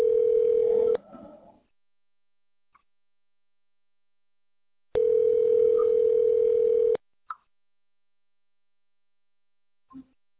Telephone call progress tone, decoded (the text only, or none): ringback tone